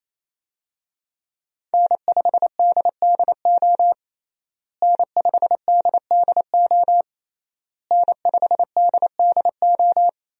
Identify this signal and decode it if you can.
Morse code, heard as a keyed tone, about 28 wpm, 700 Hz. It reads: N5DDO N5DDO N5DDO
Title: USA Call Signs - 5 Characters - Repeated 3x - 28wpm